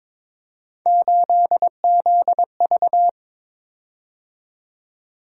Morse code, heard as a keyed tone, about 22 wpm, 700 Hz. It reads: 8ZV
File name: practice / 22wpm / Three Letter-Number Combinations - 22wpm